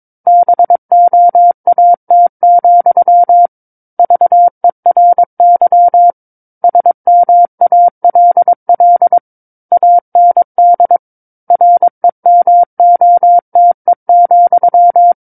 Morse code, heard as a keyed tone, about 22 wpm, 700 Hz. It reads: BOAT, VERY SMALL AND REMOTE,
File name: Book One / 13-Chapter 13 - 22 WPM